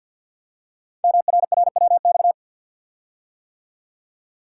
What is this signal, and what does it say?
Morse code, heard as a keyed tone, about 50 wpm, 700 Hz. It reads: MRRWX
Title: Code Groups - 5 Letters - 50wpm